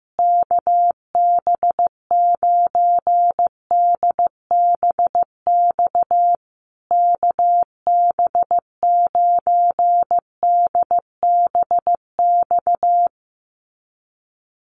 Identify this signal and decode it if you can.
Morse code, heard as a keyed tone, about 15 wpm, 700 Hz. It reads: KB9DBX KB9DBX
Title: USA Call Signs Repeated 3x 15wpm